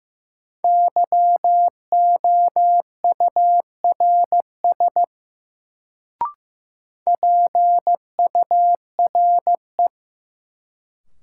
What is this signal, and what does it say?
Morse code, heard as a keyed tone, about 15 wpm, 700 Hz. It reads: YOURS PURE
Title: Sentences from Top 1500 Words - New Word Drill - 15wpm